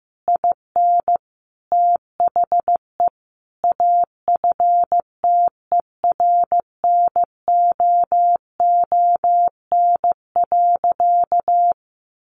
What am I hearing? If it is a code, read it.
Morse code, heard as a keyed tone, about 15 wpm, 700 Hz. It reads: IN THE AFTERNOON.
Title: Sentences from Top 1500 Words 15wpm